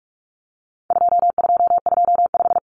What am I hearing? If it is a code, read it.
Morse code, heard as a keyed tone, about 44 wpm, 700 Hz. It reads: 2225
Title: Common Cut Numbers - 4-Digit 45wpm